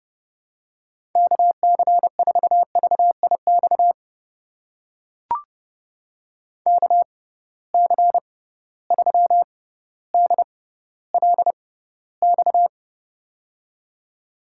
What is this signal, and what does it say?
Morse code, heard as a keyed tone, about 30 wpm, spaced as for 15 wpm, 700 Hz. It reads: KC4VIX KC3DLX
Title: USA Call Signs - 6 Characters - 6x Character Spacing - 30wpm